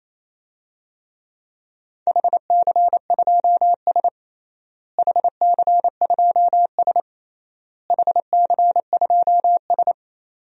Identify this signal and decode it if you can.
Morse code, heard as a keyed tone, about 27 wpm, 700 Hz. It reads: HC2S HC2S HC2S
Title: DXCC-100 Call Signs - 4 Characters - Repeated 3x - 28wpm